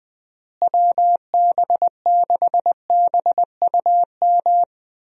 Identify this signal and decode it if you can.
Morse code, heard as a keyed tone, about 20 wpm, 700 Hz. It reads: WB6BUM
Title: USA Call Signs - 6 Characters - Repeated 2x - 20wpm